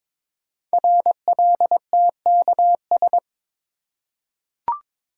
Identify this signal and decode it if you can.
Morse code, heard as a keyed tone, about 23 wpm, 700 Hz. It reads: RLTKS
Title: Code Groups - 5 Letters - 22wpm